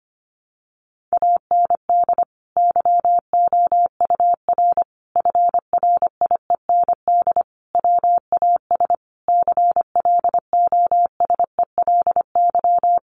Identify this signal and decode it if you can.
Morse code, heard as a keyed tone, about 25 wpm, 700 Hz. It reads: AND YOUR FRIEND WAS CLOSELY
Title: Sentences from Top 1400 Words 25wpm